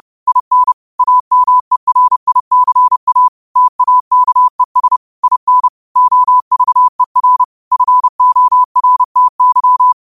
Morse code, heard as a keyed tone, about 30 wpm, 1000 Hz. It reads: IN AMERICA TAKES IN OVER FORTY
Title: 30 WPM morse code quotes for Fri, 01 Mar 2024